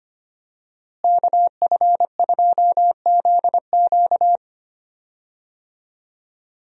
Morse code, heard as a keyed tone, about 25 wpm, 700 Hz. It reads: KF2ZQ
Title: USA Call Signs - 5 Characters - Repeated 3x - 25wpm